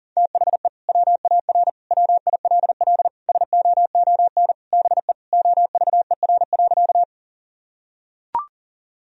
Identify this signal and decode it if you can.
Morse code, heard as a keyed tone, about 40 wpm, 700 Hz. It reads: THE WAR WILL SOON BE OVER.
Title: Sentences from Top 1000 Words - Review Part 2 - 40wpm